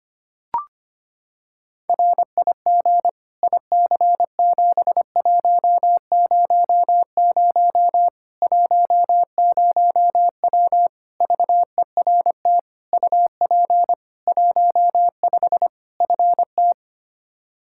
Morse code, heard as a keyed tone, about 25 wpm, 700 Hz. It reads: RIG IC7100 10W VERT UP 15 FT
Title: Practice QSO - Exchange 2 - Info - 25wpm